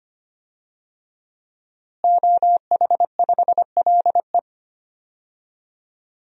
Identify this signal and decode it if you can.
Morse code, heard as a keyed tone, about 25 wpm, 700 Hz. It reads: OH5LE
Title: DXCC-100 Call Signs - 5 Characters - Repeated 2x - 25wpm